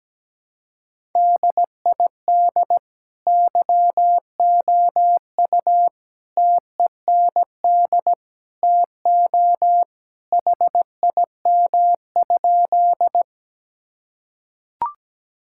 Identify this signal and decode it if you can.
Morse code, heard as a keyed tone, about 17 wpm, 700 Hz. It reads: DID YOU TEND TO HIM?
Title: Sentences from Top 1000 Words - Review Part 5 - 17wpm